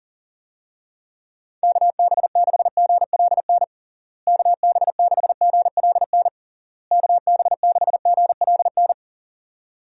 Morse code, heard as a keyed tone, about 40 wpm, 700 Hz. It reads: KB6GLN KB6GLN KB6GLN
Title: USA Call Signs - 6 Characters - Repeated 3x - 40wpm